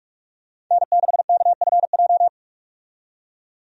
Morse code, heard as a keyed tone, about 46 wpm, 700 Hz. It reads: N6KFJ
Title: USA Call Signs - 5 Characters - 45wpm